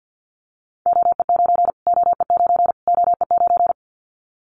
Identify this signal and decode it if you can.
Morse code, heard as a keyed tone, about 53 wpm, 700 Hz. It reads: OE9 OE9 OE9
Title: Callsign Prefixes for Top 25 DXCC - Repeated 3x - 50wpm